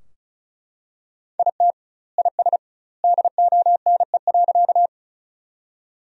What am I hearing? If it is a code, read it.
Morse code, heard as a keyed tone, about 36 wpm, 700 Hz. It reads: IT IS DONE.
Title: 2x Word-Spacing - Sentences from Top 500 Words - 35wpm